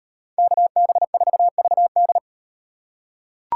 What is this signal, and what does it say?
Morse code, heard as a keyed tone, about 38 wpm, 700 Hz. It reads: KB4VD